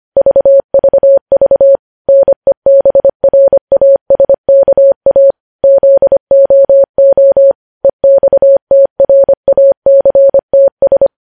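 Morse code, heard as a keyed tone, about 25 wpm, 550 Hz. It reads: VVV NEBRASKA ZOO EXTRACTS
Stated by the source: Morse code news headlines sent at 25, opening with vvv.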